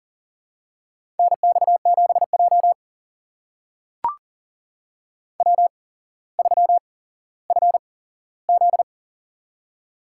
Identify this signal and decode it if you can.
Morse code, heard as a keyed tone, about 38 wpm, spaced as for 17 wpm, 700 Hz. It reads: NX7J W3FZ
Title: USA Call Signs - 4 Characters - 8x Character Spacing - 40wpm